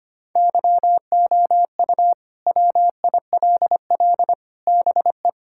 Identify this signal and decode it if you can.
Morse code, heard as a keyed tone, about 25 wpm, 700 Hz. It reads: YOU WILL BE
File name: practice / 25wpm / Sentences with 4-Word Phrases - 25wpm